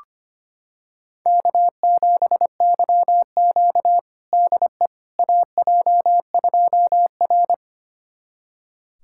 Morse code, heard as a keyed tone, about 25 wpm, 700 Hz. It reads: K7YQ DE AJ2R
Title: USA Callsign DE Callsign - 4 Characters - 25wpm